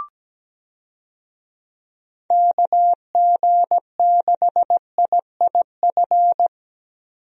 Morse code, heard as a keyed tone, about 17 wpm, 700 Hz. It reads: KG6IIF
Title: USA Call Signs - 6 Characters - 17wpm